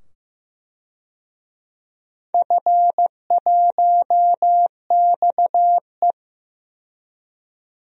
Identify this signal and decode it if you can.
Morse code, heard as a keyed tone, about 15 wpm, 700 Hz. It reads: F1XE
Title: DXCC-100 Call Signs - 4 Characters - 15wpm